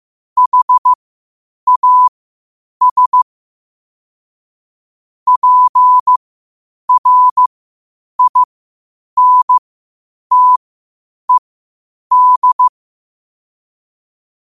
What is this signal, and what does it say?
Morse code, heard as a keyed tone, about 15 wpm, spaced as for 8 wpm, 1000 Hz. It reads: HAS PRINTED
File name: qotdfiles / 8wpm